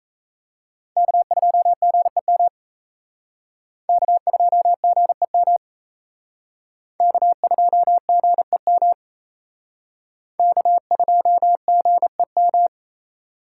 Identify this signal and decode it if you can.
Morse code, heard as a keyed tone, about 34 wpm, 700 Hz. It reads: K2GEM K2GEM K2GEM K2GEM
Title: Speed Racing - USA 1x3 Call Signs 28wpm